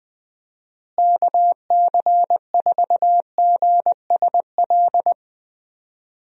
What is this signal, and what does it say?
Morse code, heard as a keyed tone, about 20 wpm, 700 Hz. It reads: KC4GSL
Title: USA Call Signs - 6 Characters - Repeated 2x - 20wpm